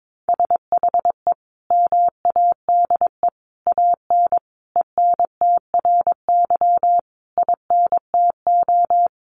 Morse code, heard as a keyed tone, about 22 wpm, 700 Hz. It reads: SHE MADE AN ENTRY INTO
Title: Sentences from Top 1500 Words 22wpm